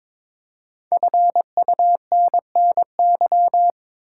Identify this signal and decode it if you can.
Morse code, heard as a keyed tone, about 22 wpm, 700 Hz. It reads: FUNNY
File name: 5 Letter Words - First 3 Letters Given - 22wpm